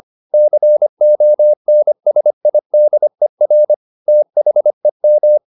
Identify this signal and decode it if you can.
Morse code, heard as a keyed tone, about 25 wpm, 600 Hz. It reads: CONSIDER THEM